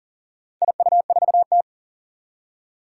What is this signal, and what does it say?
Morse code, heard as a keyed tone, about 38 wpm, 700 Hz. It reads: IU4T
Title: Four Challenging Letter-Number Combinations 40wpm